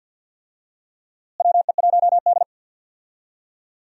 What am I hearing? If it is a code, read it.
Morse code, heard as a keyed tone, about 48 wpm, 700 Hz. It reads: WE1D
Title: USA Call Signs - 4 Characters - Repeated 3x - 50wpm